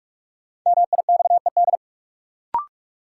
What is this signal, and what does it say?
Morse code, heard as a keyed tone, about 46 wpm, 700 Hz. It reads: MIXED
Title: Sentences from Top 1500 Words - New Word Drill - 45wpm